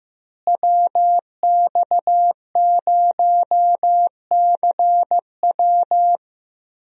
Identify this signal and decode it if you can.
Morse code, heard as a keyed tone, about 15 wpm, 700 Hz. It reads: WX0CW
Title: USA Call Signs - 5 Characters - Repeated 3x - 15wpm